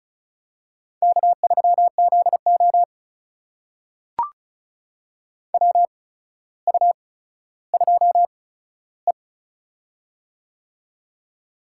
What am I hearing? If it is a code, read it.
Morse code, heard as a keyed tone, about 34 wpm, spaced as for 15 wpm, 700 Hz. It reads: K3ZO WU2E
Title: USA Call Signs - 4 Characters - 8x Character Spacing - 35wpm